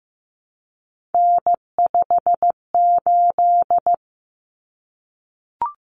Morse code, heard as a keyed tone, about 15 wpm, 700 Hz. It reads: N58
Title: Common Cut Numbers - 3-Digit 15wpm